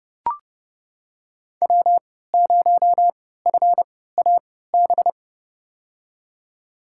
Morse code, heard as a keyed tone, about 30 wpm, spaced as for 17 wpm, 700 Hz. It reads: W0FAB
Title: USA Call Signs - 5 Characters - 3x Character Spacing - 30wpm